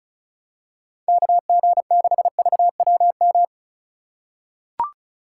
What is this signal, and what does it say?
Morse code, heard as a keyed tone, about 34 wpm, 700 Hz. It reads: KG6VWM
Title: USA Call Signs - 6 Characters - 4x Character Spacing - 35wpm